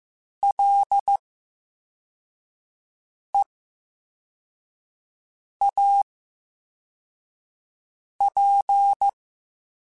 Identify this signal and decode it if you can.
Morse code, heard as a keyed tone, about 15 wpm, spaced as for 4 wpm, 800 Hz. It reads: LEAP
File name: Special INGs / Special INGs-mp3s / Slow ing @13